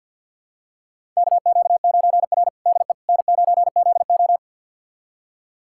Morse code, heard as a keyed tone, about 46 wpm, 700 Hz. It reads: KQ9R DE N9ZO
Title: USA Callsign DE Callsign - 4 Characters - 50wpm